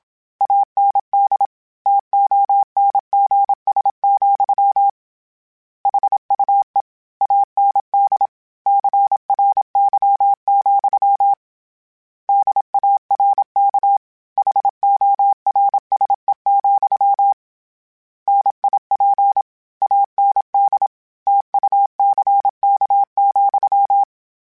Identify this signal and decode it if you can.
Morse code, heard as a keyed tone, about 27 wpm, 800 Hz. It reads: AND TONGS, HUE AND CRY, DARK HORSE, NIP AND TUCK,
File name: Misc / Misc-mp3s / Word Pairs @25